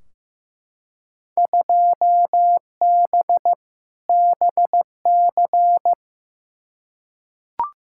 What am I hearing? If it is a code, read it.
Morse code, heard as a keyed tone, about 15 wpm, 700 Hz. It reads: 2B BC
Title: Field Day Exchanges Encore - 15wpm